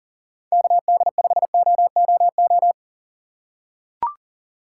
Morse code, heard as a keyed tone, about 40 wpm, 700 Hz. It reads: KD5OOO
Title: USA Call Signs - 6 Characters - 3x Character Spacing - 40wpm